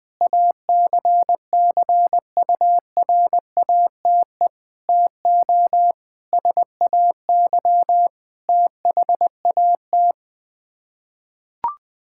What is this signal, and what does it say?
Morse code, heard as a keyed tone, about 20 wpm, 700 Hz. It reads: ACCURATE TO SAY THAT
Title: Sets of 8 Words - 20wpm